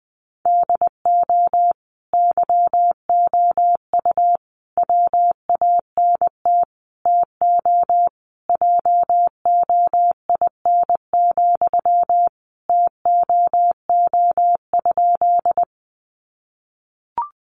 Morse code, heard as a keyed tone, about 20 wpm, 700 Hz. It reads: DO YOU WANT TO JOIN, TOO?
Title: Sentences from Top 1400 Words 20wpm